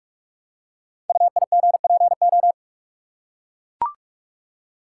Morse code, heard as a keyed tone, about 44 wpm, 700 Hz. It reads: UIGPO
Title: Code Groups - 5 Letters - 45wpm